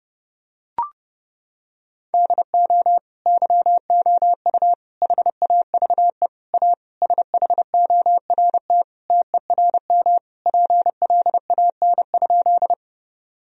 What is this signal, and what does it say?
Morse code, heard as a keyed tone, about 32 wpm, 700 Hz. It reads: DO YOU HAVE A SHORT TERM PLAN?
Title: Sentences from Top 1000 Words - Review Part 4 - 30wpm